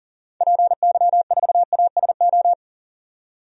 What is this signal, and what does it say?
Morse code, heard as a keyed tone, about 40 wpm, 700 Hz. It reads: PY4ASO
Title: DXCC-100 Call Signs - 6 Characters - 40wpm